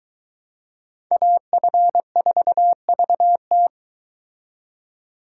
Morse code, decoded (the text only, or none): AF4VT